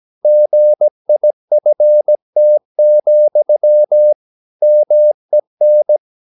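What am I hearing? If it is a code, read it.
Morse code, heard as a keyed tone, about 17 wpm, 600 Hz. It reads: GIFT, MEN
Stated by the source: Slow (QRS) Morse practice at 17 wpm, from CW med Gnister.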